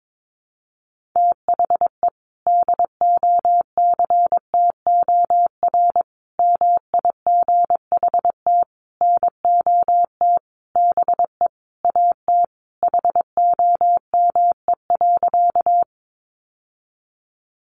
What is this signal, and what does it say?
Morse code, heard as a keyed tone, about 22 wpm, 700 Hz. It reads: THE DOCTOR MIGHT NOT BE AT HOME.
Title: Sentences from Top 1500 Words 22wpm